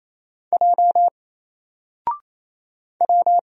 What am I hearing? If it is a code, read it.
Morse code, heard as a keyed tone, about 27 wpm, 700 Hz. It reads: J W
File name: Single-Letter-Number-28wpm